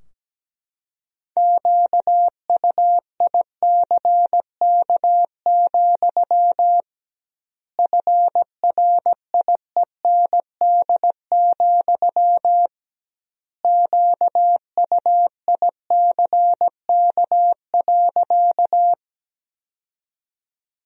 Morse code, heard as a keyed tone, about 17 wpm, 700 Hz. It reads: QUICK, FRIEND, QUICK.
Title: 2x Word-Spacing - Sentences from Top 500 Words - 17wpm